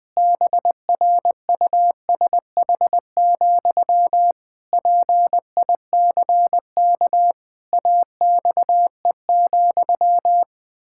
Morse code, heard as a keyed tone, about 20 wpm, 700 Hz. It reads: BRUSH, PICK AXE,
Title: Pangram Challenge v2 - 20wpm